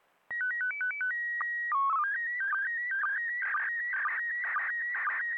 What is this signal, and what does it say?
SSTV (slow-scan television), PD 120.
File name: SSTV ISS